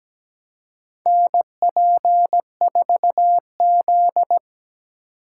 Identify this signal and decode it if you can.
Morse code, heard as a keyed tone, about 17 wpm, 700 Hz. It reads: NP4Z
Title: USA Call Signs - 2x Character Spacing - 17wpm